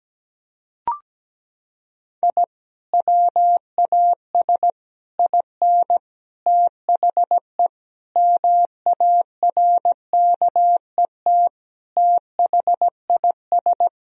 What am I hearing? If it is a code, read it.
Morse code, heard as a keyed tone, about 17 wpm, 700 Hz. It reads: I WAS IN THE MARKET THIS
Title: Sentences from Top 1000 Words - Review Part 3 - 17wpm